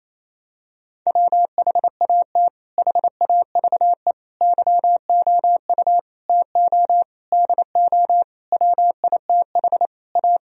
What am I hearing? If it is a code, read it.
Morse code, heard as a keyed tone, about 28 wpm, 700 Hz. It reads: WHAT HAVE YOU TO DO WITH A
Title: Sentences from Top 1000 Words - Review Part 2 - 28wpm